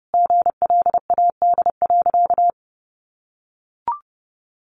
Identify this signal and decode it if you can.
Morse code, heard as a keyed tone, about 30 wpm, 700 Hz. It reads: GLAD.
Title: Sentences from Top 800 Words 30wpm